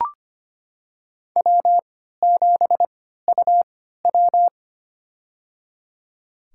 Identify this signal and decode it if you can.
Morse code, heard as a keyed tone, about 25 wpm, spaced as for 14 wpm, 700 Hz. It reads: W7UW
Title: USA Call Signs - 4 Characters - 3x Character Spacing - 25wpm